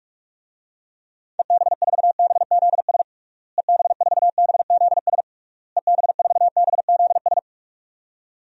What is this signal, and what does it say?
Morse code, heard as a keyed tone, about 46 wpm, 700 Hz. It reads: EB4BZS EB4BZS EB4BZS
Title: DXCC-100 Call Signs - 6 Characters - Repeated 3x - 45wpm